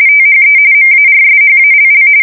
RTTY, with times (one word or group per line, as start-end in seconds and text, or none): 0.1-0.6 THE
0.7-2.1 MARSHALL